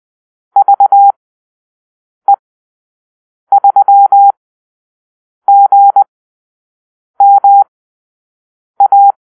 Morse code, heard as a keyed tone, about 20 wpm, spaced as for 6 wpm, 800 Hz. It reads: VE3GMA
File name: arrl / Call-Sign Drill-6